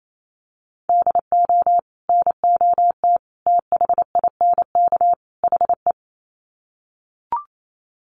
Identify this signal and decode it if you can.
Morse code, heard as a keyed tone, about 28 wpm, 700 Hz. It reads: DO NOT THINK HE
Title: Sets of 4 Words 28wpm